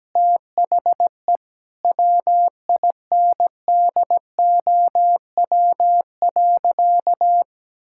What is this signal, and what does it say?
Morse code, heard as a keyed tone, about 17 wpm, 700 Hz. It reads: THE WINDOW.
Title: Sentences from Top 1600 Words 17wpm